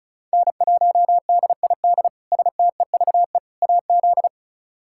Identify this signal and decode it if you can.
Morse code, heard as a keyed tone, about 36 wpm, 700 Hz. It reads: N1DID STEVE AZ
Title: K1USN SST QSO Practice - 35wpm